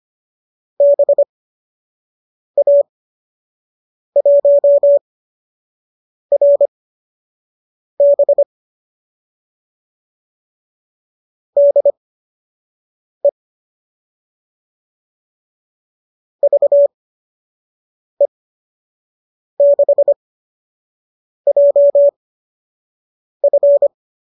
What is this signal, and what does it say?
Morse code, heard as a keyed tone, about 25 wpm, spaced as for 6 wpm, 570 Hz. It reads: BA1RB DE VE6JF
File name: qso002 06